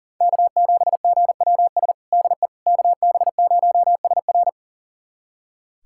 Morse code, heard as a keyed tone, about 40 wpm, 700 Hz. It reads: K7GWS DE KB0SR
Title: USA Callsign DE Callsign - 5 Characters - 40wpm